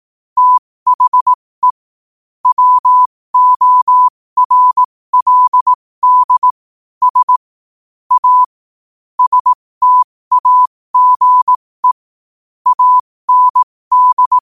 Morse code, heard as a keyed tone, about 18 wpm, spaced as for 15 wpm, 1000 Hz. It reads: THE WORLD S A STAGE AND
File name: qotdfiles / 15wpm